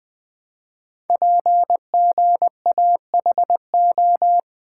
Morse code, heard as a keyed tone, about 21 wpm, 700 Hz. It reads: PGAHO
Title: Code Groups - 5 Letters - 20wpm